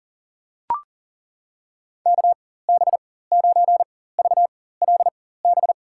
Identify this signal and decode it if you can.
Morse code, heard as a keyed tone, about 40 wpm, spaced as for 19 wpm, 700 Hz. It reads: KB9VLB
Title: USA Call Signs - 6 Characters - 4x Character Spacing - 40wpm